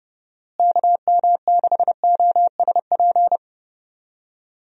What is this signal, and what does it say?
Morse code, heard as a keyed tone, about 31 wpm, 700 Hz. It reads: KM6OSP
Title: USA Call Signs - 6 Characters - Repeated 2x - 30wpm